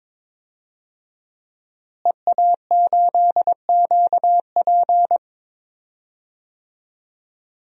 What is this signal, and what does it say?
Morse code, heard as a keyed tone, about 21 wpm, 700 Hz. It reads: EA8QP
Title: DXCC-100 Call Signs - 5 Characters - Repeated 3x - 22wpm